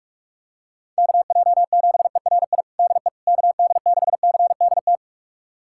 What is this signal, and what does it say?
Morse code, heard as a keyed tone, about 46 wpm, 700 Hz. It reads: KJ7ERI DE KD6CDT
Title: USA Callsign DE Callsign 45wpm